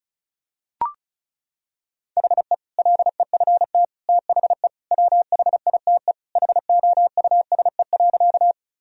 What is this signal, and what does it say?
Morse code, heard as a keyed tone, about 36 wpm, 700 Hz. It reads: HE LEFT THE WHITE HOUSE.
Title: Sentences from Top 1000 Words - Review Part 1 - 35wpm